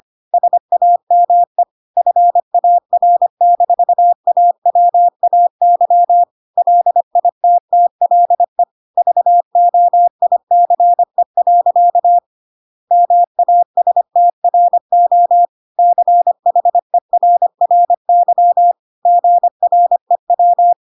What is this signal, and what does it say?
Morse code, heard as a keyed tone, about 25 wpm, 700 Hz. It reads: SAME FAR-AWAY LITTLE VOICE. MASTRO CHERRY GREW
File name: Pinocchio / Pinocchio ch1 25 25 ik1hge